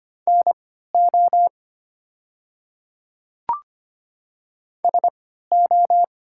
Morse code, heard as a keyed tone, about 25 wpm, spaced as for 14 wpm, 700 Hz. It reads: NO SO
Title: 3x Character Spacing - Top 100 Words - 25wpm